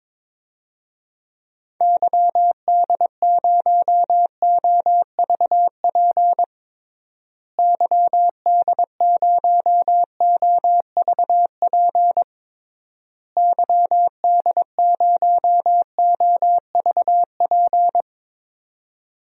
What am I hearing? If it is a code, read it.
Morse code, heard as a keyed tone, about 22 wpm, 700 Hz. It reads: YD0OVP YD0OVP YD0OVP
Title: DXCC-100 Call Signs - 6 Characters - Repeated 3x - 22wpm